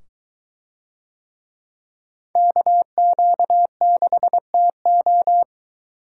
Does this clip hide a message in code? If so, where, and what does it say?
Yes: Morse code from 2.4 to 5.4 s, heard as a keyed tone, about 22 wpm, 700 Hz. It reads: KQ6TO